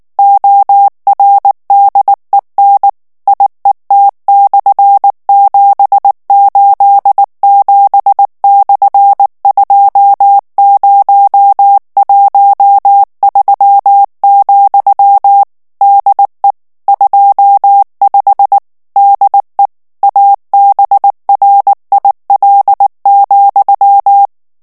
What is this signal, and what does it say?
Morse code, heard as a keyed tone, about 19 wpm, 800 Hz. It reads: ORDEN IET/787/2013, DE 25 DE ABRIL,
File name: ESPAGNOL TEXTE EN CLAIR 2 VITESSE 18 MOTS MINUTES